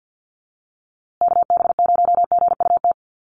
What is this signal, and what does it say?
Morse code, heard as a keyed tone, about 50 wpm, 700 Hz. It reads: KB0GUT
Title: USA Call Signs Repeated 2x 50wpm